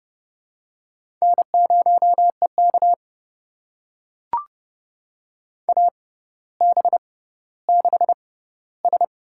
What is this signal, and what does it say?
Morse code, heard as a keyed tone, about 30 wpm, spaced as for 15 wpm, 700 Hz. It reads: N0EK AB6S
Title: USA Call Signs - 6x Character Spacing - 30wpm